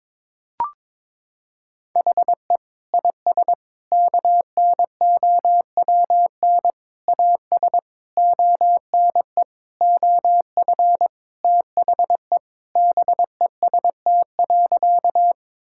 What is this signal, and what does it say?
Morse code, heard as a keyed tone, about 22 wpm, 700 Hz. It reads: HE IS KNOWN AS ONE OF THE BEST.
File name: practice / 22wpm / Sentences with 4-Word Phrases - 22wpm